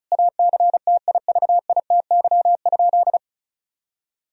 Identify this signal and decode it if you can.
Morse code, heard as a keyed tone, about 36 wpm, 700 Hz. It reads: ACTIVITY?
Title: Sentences from Top 1000 Words - Review Part 4 - 35wpm